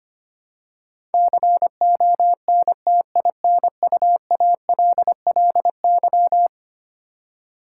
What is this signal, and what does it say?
Morse code, heard as a keyed tone, about 25 wpm, 700 Hz. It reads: CONTINUALLY
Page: morse-code-ninja-practice-25wpm